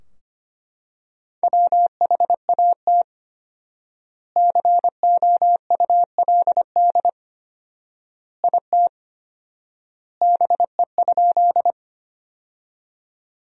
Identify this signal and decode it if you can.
Morse code, heard as a keyed tone, about 25 wpm, 700 Hz. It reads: WHAT COULD IT BE?
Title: 4x Word-Spacing - Sentences from Top 100 Words - 25wpm